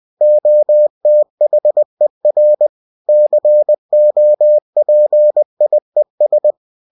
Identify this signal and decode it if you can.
Morse code, heard as a keyed tone, about 20 wpm, 600 Hz. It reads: OTHER COPIES
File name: cw-books / winnie-the-pooh / winnie-20wpm-page-0117